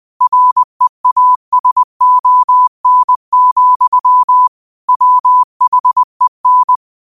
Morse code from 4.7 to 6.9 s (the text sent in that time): WHEN